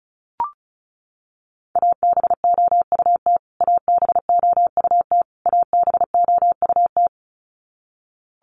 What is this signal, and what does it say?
Morse code, heard as a keyed tone, about 34 wpm, 700 Hz. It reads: ABOUT ABOUT ABOUT
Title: Top 500 Words Repeated 35wpm